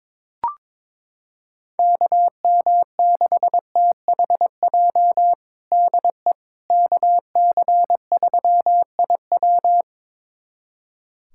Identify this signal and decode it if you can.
Morse code, heard as a keyed tone, about 22 wpm, 700 Hz. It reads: KM6THJ DE KC3IW
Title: USA Callsign DE Callsign 22wpm